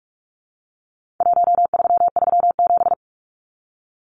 Morse code, heard as a keyed tone, about 44 wpm, 700 Hz. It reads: 1337
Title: Numbers - 4-Digit 45wpm